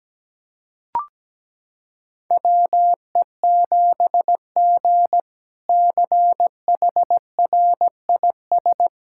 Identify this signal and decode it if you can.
Morse code, heard as a keyed tone, about 17 wpm, 700 Hz. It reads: WE7G CHRIS
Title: K1USN SST QSO Practice - 17wpm